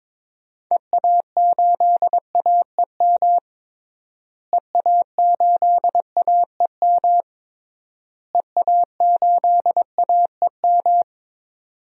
Morse code, heard as a keyed tone, about 22 wpm, 700 Hz. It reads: EA8AEM EA8AEM EA8AEM
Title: DXCC-100 Call Signs - 6 Characters - Repeated 3x - 22wpm